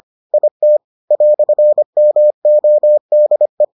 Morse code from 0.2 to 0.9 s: IT

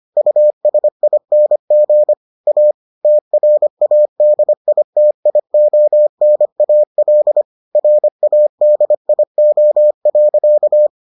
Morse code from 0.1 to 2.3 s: USING

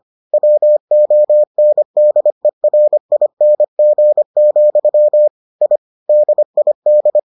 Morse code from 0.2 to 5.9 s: WONDERING, I

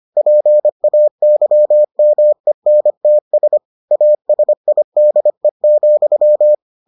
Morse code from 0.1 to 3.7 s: PAYMENTS

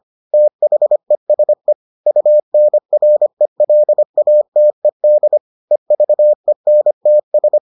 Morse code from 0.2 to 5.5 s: THESE UNRELATED

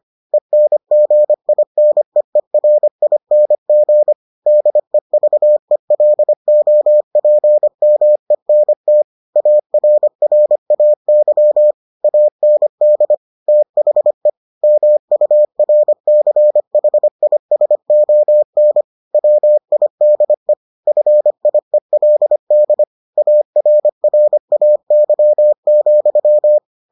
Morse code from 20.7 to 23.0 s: FIELD

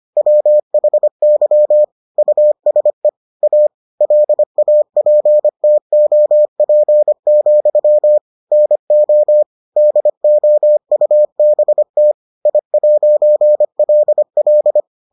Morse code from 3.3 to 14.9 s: A LAPTOP, NO DOUBT I'LL

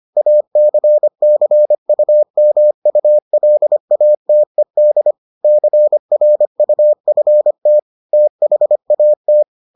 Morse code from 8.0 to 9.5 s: THAT

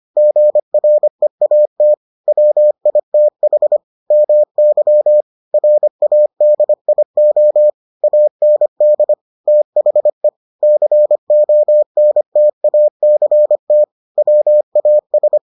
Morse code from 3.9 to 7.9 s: MY RADIO

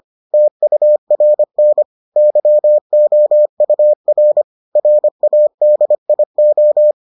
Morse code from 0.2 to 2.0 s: TURN

